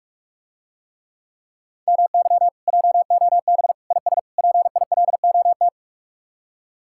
Morse code, heard as a keyed tone, about 45 wpm, 700 Hz. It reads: MY JOB IS PILOT